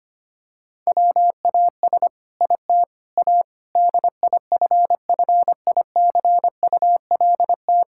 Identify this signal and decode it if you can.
Morse code, heard as a keyed tone, about 25 wpm, 700 Hz. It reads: WAS IT A DIFFICULT
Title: Sentences from Top 1000 Words - Review Part 6 - 25wpm